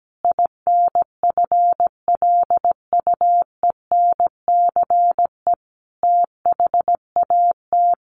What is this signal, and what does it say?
Morse code, heard as a keyed tone, about 17 wpm, 700 Hz. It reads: INFLUENCE THAT
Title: Sentences from Top 1100 Words 17wpm